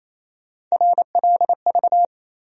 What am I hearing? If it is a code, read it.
Morse code, heard as a keyed tone, about 28 wpm, 700 Hz. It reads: RLV
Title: USA Call Sign Postfixes - 3 Letters 28wpm